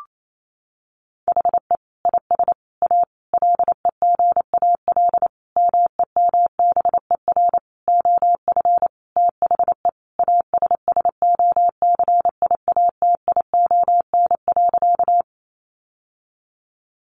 Morse code, heard as a keyed tone, about 28 wpm, 700 Hz. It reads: HE IS A LEGAL MEMBER OF THE ASSOCIATION.
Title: Sentences from Top 1200 Words 28wpm